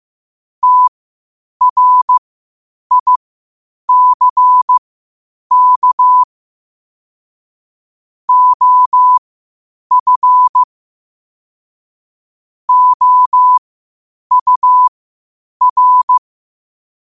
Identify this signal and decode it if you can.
Morse code, heard as a keyed tone, about 15 wpm, spaced as for 8 wpm, 1000 Hz. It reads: TRICK OF OUR
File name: qotdfiles / 8wpm